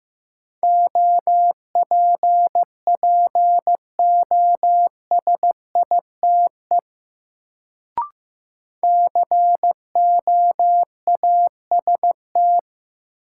Morse code, heard as a keyed tone, about 15 wpm, 700 Hz. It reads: OPPOSITE COAST
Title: Sentences from Top 1300 Words - New Word Drill Alt - 15wpm